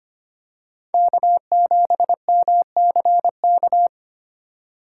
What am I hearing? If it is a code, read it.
Morse code, heard as a keyed tone, about 25 wpm, 700 Hz. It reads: K7MCK
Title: USA Call Signs - 3x Character Spacing - 25wpm